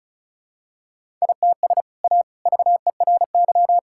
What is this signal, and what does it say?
Morse code, heard as a keyed tone, about 34 wpm, 700 Hz. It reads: ITS A VERY
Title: Sentences from Top 1000 Words - Review Part 5 - 35wpm